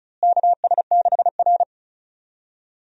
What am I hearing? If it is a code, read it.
Morse code, heard as a keyed tone, about 36 wpm, 700 Hz. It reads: KS6R
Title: USA Call Signs - 4 Characters - Repeated 2x - 35wpm